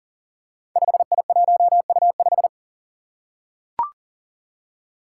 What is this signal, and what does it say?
Morse code, heard as a keyed tone, about 41 wpm, 700 Hz. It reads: 5I1U5
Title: Five Challenging Letter-Number Combinations 40wpm